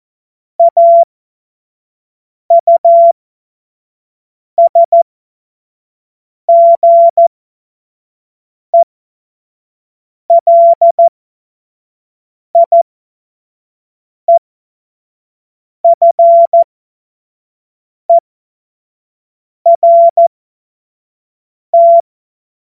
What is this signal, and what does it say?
Morse code, heard as a keyed tone, about 14 wpm, spaced as for 5 wpm, 680 Hz. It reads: AUSGELIEFERT
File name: Morsekurs / Witze / Witz 22. August 2023 25 F70